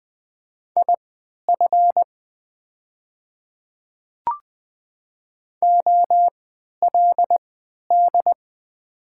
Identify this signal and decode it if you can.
Morse code, heard as a keyed tone, about 20 wpm, spaced as for 11 wpm, 690 Hz. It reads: IF OLD
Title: 3x Character Spacing - Top 200 Words - 20wpm